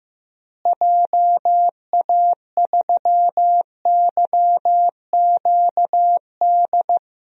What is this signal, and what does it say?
Morse code, heard as a keyed tone, about 15 wpm, 700 Hz. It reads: JA3YQD
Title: DXCC-100 Call Signs - 6 Characters - Repeated 3x - 15wpm